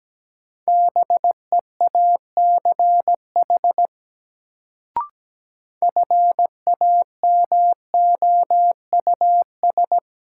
Morse code, heard as a keyed tone, about 17 wpm, 700 Hz. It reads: BEACH FAMOUS
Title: Sentences from Top 1500 Words - New Word Drill Alt - 17wpm